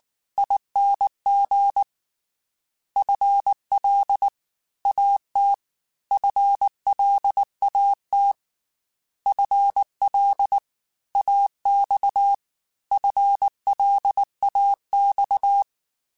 Morse code, heard as a keyed tone, about 19 wpm, 800 Hz. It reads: ING FL AT FLAT FL AX FLAX
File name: FL / FL mp3s / Fast FL @18